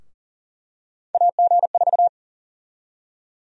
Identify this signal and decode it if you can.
Morse code, heard as a keyed tone, about 40 wpm, 700 Hz. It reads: AG4
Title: USA Call Sign Prefixes - Repeated 2x - 40wpm